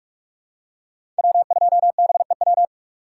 Morse code, heard as a keyed tone, about 46 wpm, 700 Hz. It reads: W2BEW
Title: USA Call Signs - 5 Characters - 4x Character Spacing - 45wpm